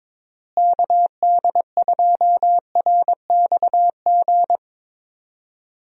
Morse code, heard as a keyed tone, about 22 wpm, 700 Hz. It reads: KD2RXG
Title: USA Call Signs - 6 Characters - Repeated 2x - 22wpm